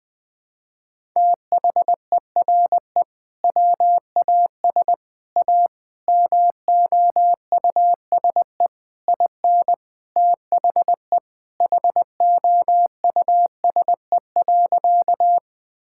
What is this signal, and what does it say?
Morse code, heard as a keyed tone, about 20 wpm, 700 Hz. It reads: THERE WAS A MOUSE IN THE HOUSE.
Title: Sentences from Top 1000 Words - Review Part 5 - 20wpm